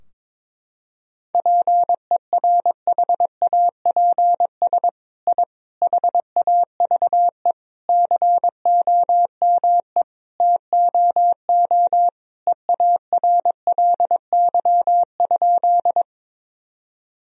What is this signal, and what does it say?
Morse code, heard as a keyed tone, about 22 wpm, 700 Hz. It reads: PERHAPS I HAVE COME TOO EARLY?
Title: Sentences from Top 1000 Words - Review Part 2 - 22wpm